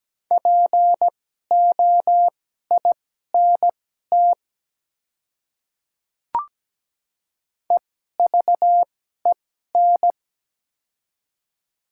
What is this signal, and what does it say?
Morse code, heard as a keyed tone, about 17 wpm, spaced as for 12 wpm, 700 Hz. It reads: POINT EVEN
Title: 2x Character Spacing - Top 200 Words - 17wpm